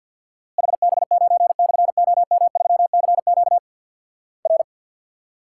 Morse code, heard as a keyed tone, about 50 wpm, 700 Hz. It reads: HB9/OM3CQ R
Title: Worked All Europe DX Contest - QTC Exchange with Repeats - 50wpm